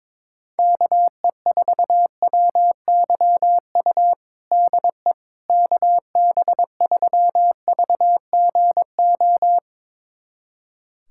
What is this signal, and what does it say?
Morse code, heard as a keyed tone, about 22 wpm, 700 Hz. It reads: KE4WYU DE KB3VGO
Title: USA Callsign DE Callsign - 6 Characters - 22wpm